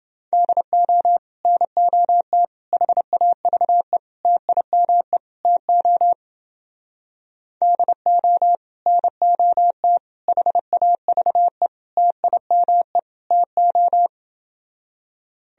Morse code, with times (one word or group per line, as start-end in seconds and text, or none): 0.3-1.2 DO
1.4-2.4 NOT
2.7-4.0 HAVE
4.2-5.2 TIME
5.4-6.1 TO
7.6-8.6 DO
8.9-10.0 NOT
10.3-11.7 HAVE
12.0-13.0 TIME
13.3-14.1 TO